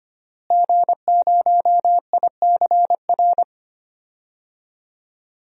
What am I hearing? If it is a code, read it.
Morse code, heard as a keyed tone, about 25 wpm, 700 Hz. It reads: G0ICR
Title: DXCC-100 Call Signs - 5 Characters - Repeated 3x - 25wpm